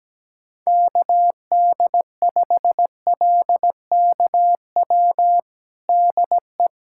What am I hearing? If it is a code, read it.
Morse code, heard as a keyed tone, about 17 wpm, 700 Hz. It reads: KD5LKW DE
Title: USA Callsign DE Callsign - 6 Characters - 17wpm